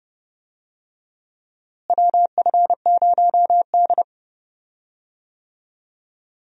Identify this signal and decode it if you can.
Morse code, heard as a keyed tone, about 30 wpm, 700 Hz. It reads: WF0D
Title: USA Call Signs - 4 Characters - Repeated 3x - 30wpm